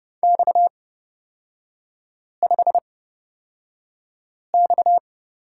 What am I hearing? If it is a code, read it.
Morse code, heard as a keyed tone, about 30 wpm, spaced as for 5 wpm, 700 Hz. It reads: X5X
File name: Lesson 37e - Letter x - Three Characters 30wpm